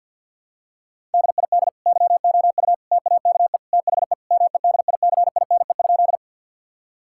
Morse code, heard as a keyed tone, about 50 wpm, 700 Hz. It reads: DID YOU TAKE THE MEDICINE?
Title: Sentences from Top 1000 Words - Review Part 6 - 50wpm